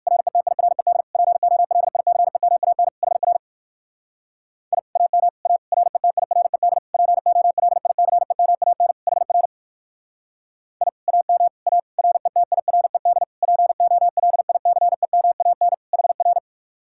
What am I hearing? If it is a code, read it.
Morse code, heard as a keyed tone, about 59 wpm, 700 Hz. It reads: RETIRED POLICEMAN HR I AM A RETIRED POLICEMAN HR I AM A RETIRED POLICEMAN HR